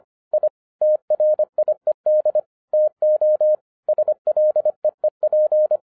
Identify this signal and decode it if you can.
Morse code, heard as a keyed tone, about 25 wpm, 610 Hz. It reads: I TRIED TO SLEEP
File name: 2011-07-mp3 / n34 THE 3.11 2 25wpm